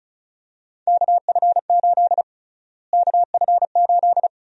Morse code, heard as a keyed tone, about 34 wpm, 700 Hz. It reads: KF8 KF8
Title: USA Call Sign Prefixes - Repeated 2x - 35wpm